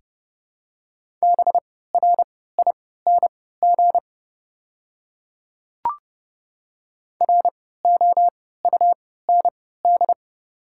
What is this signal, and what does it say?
Morse code, heard as a keyed tone, about 32 wpm, spaced as for 17 wpm, 700 Hz. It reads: BRING ROUND
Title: 3x Character Spacing - Top 500 Words - 30wpm